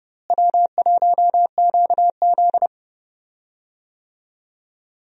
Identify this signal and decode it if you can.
Morse code, heard as a keyed tone, about 32 wpm, 700 Hz. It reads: W1QZ
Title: USA Call Signs - 4 Characters - Repeated 2x - 30wpm